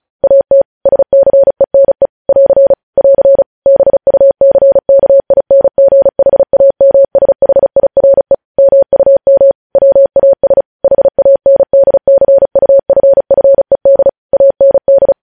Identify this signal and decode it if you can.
Morse code, heard as a keyed tone, about 36 wpm, 550 Hz. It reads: AT SCENE <AR> <AR> BUCKINGHAMSHIRE MUM WAS HANDCUFFED AND
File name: podcast / News headlines-1x-35x35